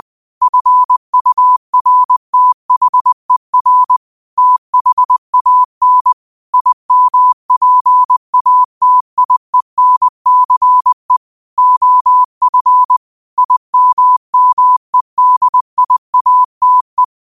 Morse code, heard as a keyed tone, about 20 wpm, 1000 Hz. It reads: FURTHER THAN IMPATIENCE OF IMMEDIATE
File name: qotdfiles / 20wpm